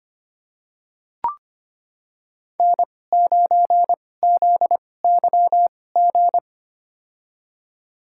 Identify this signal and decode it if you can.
Morse code, heard as a keyed tone, about 25 wpm, spaced as for 18 wpm, 700 Hz. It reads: N9ZYG
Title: USA Call Signs - 2x Character Spacing - 25wpm